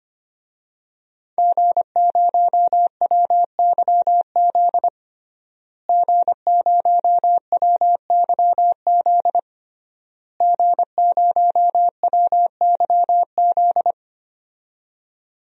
Morse code, heard as a keyed tone, about 25 wpm, 700 Hz. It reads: G0WYZ G0WYZ G0WYZ
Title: DXCC-100 Call Signs - 5 Characters - Repeated 3x - 25wpm